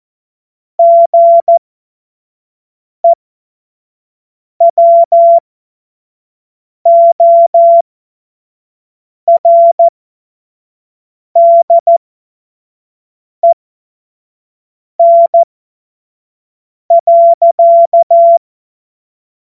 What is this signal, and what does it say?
Morse code, heard as a keyed tone, about 14 wpm, spaced as for 5 wpm, 680 Hz. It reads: GEWORDEN.
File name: Morsekurs / Witze / Witz 23. August 2022 25 F70